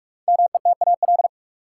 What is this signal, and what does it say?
Morse code, heard as a keyed tone, about 46 wpm, 700 Hz. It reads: METAL
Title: Sentences from Top 1000 Words - Review Part 6 - New Word Drill - 45wpm